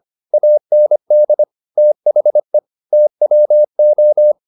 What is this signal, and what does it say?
Morse code, heard as a keyed tone, about 25 wpm, 600 Hz. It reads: AND THE TWO